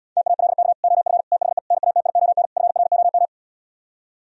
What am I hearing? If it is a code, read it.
Morse code, heard as a keyed tone, about 77 wpm, 700 Hz. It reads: TALK OF THE NATION FROM